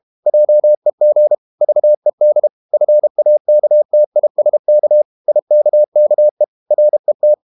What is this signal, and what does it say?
Morse code, heard as a keyed tone, about 32 wpm, 600 Hz. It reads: JEG VED FAKTISK IKKE RET
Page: Ferie 32 wpm